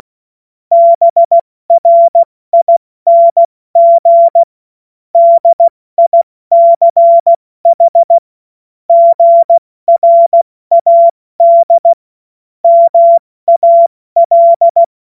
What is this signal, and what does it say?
Morse code, heard as a keyed tone, about 16 wpm, 680 Hz. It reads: BRING DICH GRAD MAL